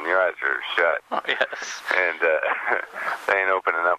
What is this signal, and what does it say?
Emergency Alert System (SAME) tones, heard as harsh buzzing bursts, with speech. It is heard from 0.3 to 3.0 s.